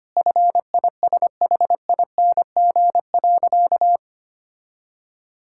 Morse code, heard as a keyed tone, about 25 wpm, 700 Hz. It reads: FISHING.
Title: Sentences from Top 1000 Words - Review Part 5 - 25wpm